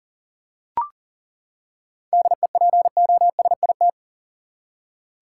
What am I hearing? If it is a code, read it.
Morse code, heard as a keyed tone, about 40 wpm, 700 Hz. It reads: DEPOSIT
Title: Seven Letter Words - 40wpm